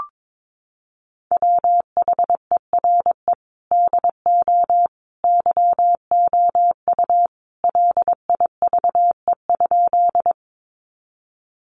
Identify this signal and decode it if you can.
Morse code, heard as a keyed tone, about 22 wpm, 700 Hz. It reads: WHERE DO YOU LIVE?
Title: Sentences from Top 200 Words 22wpm